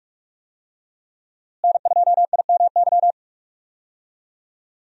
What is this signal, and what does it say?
Morse code, heard as a keyed tone, about 44 wpm, 700 Hz. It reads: N2IMY
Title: USA Call Signs - 5 Characters - Repeated 2x - 45wpm